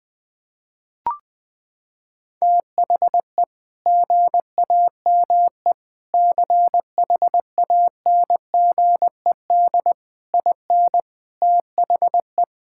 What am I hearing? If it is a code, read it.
Morse code, heard as a keyed tone, about 20 wpm, 700 Hz. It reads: THE GAME CHANGED IN THE
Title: Sentences with 5-Word Phrases - 20wpm